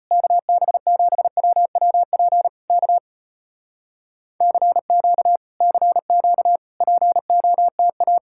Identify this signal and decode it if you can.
Morse code, heard as a keyed tone, about 33 wpm, 700 Hz. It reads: KB7WWP K CQ CQ POTA